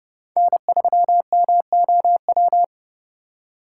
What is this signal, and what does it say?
Morse code, heard as a keyed tone, about 29 wpm, 700 Hz. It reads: N3MOW